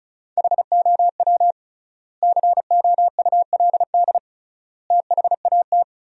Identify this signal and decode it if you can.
Morse code, heard as a keyed tone, about 34 wpm, 700 Hz. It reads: HOW COULD THAT
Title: 3x Word-Spacing - Sentences from Top 400 Words - 35wpm